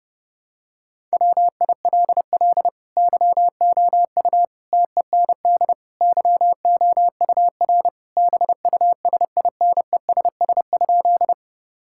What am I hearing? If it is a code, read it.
Morse code, heard as a keyed tone, about 32 wpm, 700 Hz. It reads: WILL YOU TEND YOUR BUSINESS?
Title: Sentences from Top 1000 Words - Review Part 5 - 30wpm